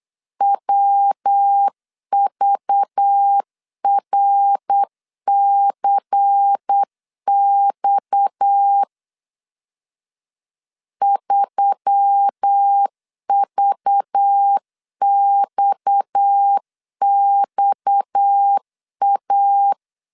Morse code, heard as a keyed tone, about 8 wpm, 800 Hz. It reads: WVRCX 3VXXA
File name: LEZIONI / 6 PKX0 / 6 PKX 8wpm 3041